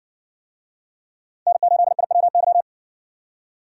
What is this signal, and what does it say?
Morse code, heard as a keyed tone, about 57 wpm, 700 Hz. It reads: N8IWY